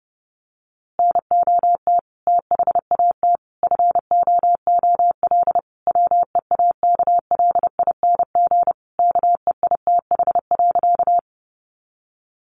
Morse code, heard as a keyed tone, about 30 wpm, 700 Hz. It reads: NOT THAT FOOL WEAKLING KEITH.
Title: A Sound Of Thunder 30wpm